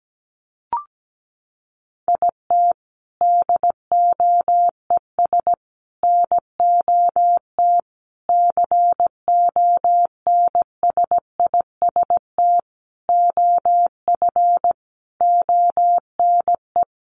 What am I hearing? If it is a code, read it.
Morse code, heard as a keyed tone, about 17 wpm, 700 Hz. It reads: IT DOES NOT CONSIST OF ONE
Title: Sentences from Top 1400 Words 17wpm